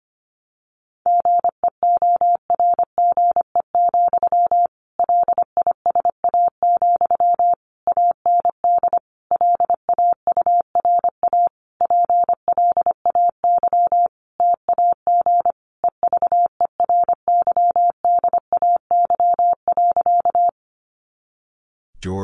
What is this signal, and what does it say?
Morse code, heard as a keyed tone, about 25 wpm, 700 Hz. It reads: GEORGE, LISA, AND LAURA PLAY TAG EVERYDAY.
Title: Additional Easy Phrases 25wpm